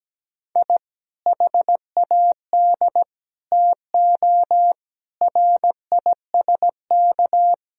Morse code, heard as a keyed tone, about 17 wpm, 700 Hz. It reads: I HAD TO RISK
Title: Sentences from Top 1000 Words - Review Part 4 - 17wpm